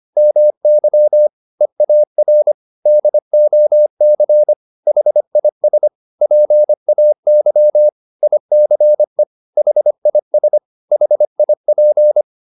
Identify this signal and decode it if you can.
Morse code, heard as a keyed tone, about 25 wpm, 600 Hz. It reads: MY EAR DOC HIS PAY ICE HIS HIP